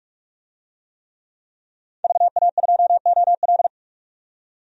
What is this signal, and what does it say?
Morse code, heard as a keyed tone, about 43 wpm, 700 Hz. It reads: VA2OL
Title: DXCC-100 Call Signs - 5 Characters - 45wpm